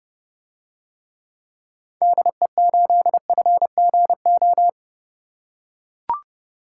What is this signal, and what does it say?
Morse code, heard as a keyed tone, about 30 wpm, 700 Hz. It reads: DE8FGO